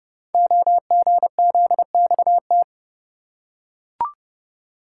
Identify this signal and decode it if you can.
Morse code, heard as a keyed tone, about 30 wpm, 700 Hz. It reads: OGZXT
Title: Code Groups - 5 Letters - 30wpm